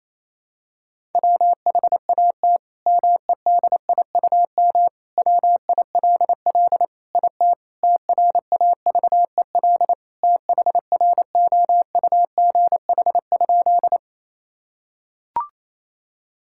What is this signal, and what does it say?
Morse code, heard as a keyed tone, about 28 wpm, 700 Hz. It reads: WHAT MEDIUM WILL IT TRAVEL THROUGH?
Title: Sentences from Top 1000 Words - Review Part 6 - 28wpm